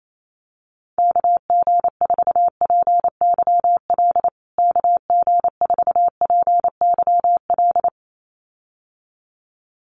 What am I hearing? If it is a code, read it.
Morse code, heard as a keyed tone, about 28 wpm, 700 Hz. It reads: KG4PYL KG4PYL
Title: USA Call Signs Repeated 2x 28wpm